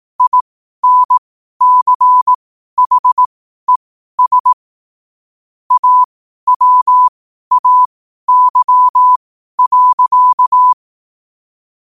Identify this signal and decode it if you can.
Morse code, heard as a keyed tone, about 18 wpm, spaced as for 12 wpm, 1000 Hz. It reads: INCHES AWAY.